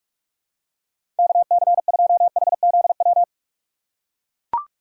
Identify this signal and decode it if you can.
Morse code, heard as a keyed tone, about 42 wpm, 700 Hz. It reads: KC2HZW
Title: USA Call Signs - 6 Characters - 6x Character Spacing - 45wpm